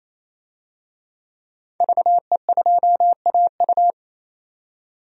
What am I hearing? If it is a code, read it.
Morse code, heard as a keyed tone, about 28 wpm, 700 Hz. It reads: VE2AU
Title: DXCC-100 Call Signs - 5 Characters - 28wpm